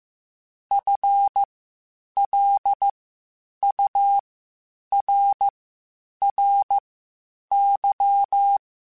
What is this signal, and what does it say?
Morse code, heard as a keyed tone, about 15 wpm, spaced as for 8 wpm, 800 Hz. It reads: FLURRY